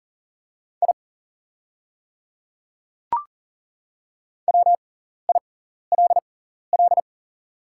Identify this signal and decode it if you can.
Morse code, heard as a keyed tone, about 40 wpm, spaced as for 14 wpm, 700 Hz. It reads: I WILL
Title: 6x Character Spacing - Top 100 Words - 40wpm